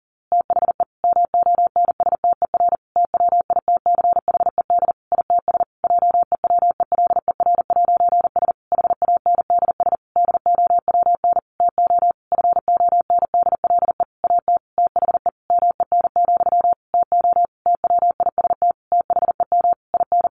Morse code, heard as a keyed tone, about 40 wpm, 700 Hz. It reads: THE MONSTER TWITCHED ITS JEWELER'S HANDS DOWN TO FONDLE AT THE MEN, TO TWIST THEM IN
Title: A Sound Of Thunder 40wpm